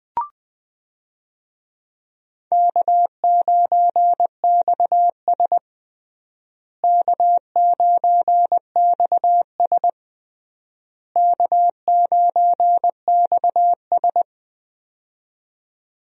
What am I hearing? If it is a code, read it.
Morse code, heard as a keyed tone, about 20 wpm, 700 Hz. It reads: K9XS K9XS K9XS
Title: USA Call Signs - 4 Characters - Repeated 3x - 20wpm